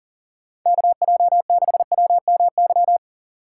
Morse code, heard as a keyed tone, about 40 wpm, 700 Hz. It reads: KJ6WMY
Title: USA Call Signs - 6 Characters - 3x Character Spacing - 40wpm